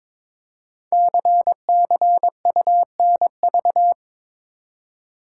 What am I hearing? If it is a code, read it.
Morse code, heard as a keyed tone, about 22 wpm, 700 Hz. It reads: CCUNV
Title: Code Groups - 5 Letters - 22wpm